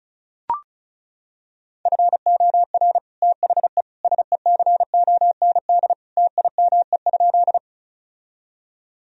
Morse code, heard as a keyed tone, about 34 wpm, 700 Hz. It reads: FOR THE SECOND TIME?
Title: Sentences from Top 1000 Words - Review Part 1 - 35wpm